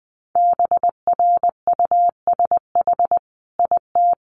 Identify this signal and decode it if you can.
Morse code, heard as a keyed tone, about 20 wpm, 700 Hz. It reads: BRUSH IT
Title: Sentences from Top 1100 Words 20wpm